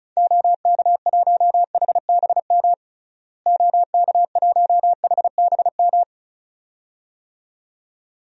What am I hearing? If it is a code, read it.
Morse code, heard as a keyed tone, about 36 wpm, 700 Hz. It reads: OK1HBM OK1HBM
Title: DXCC-100 Call Signs - 6 Characters - Repeated 2x - 35wpm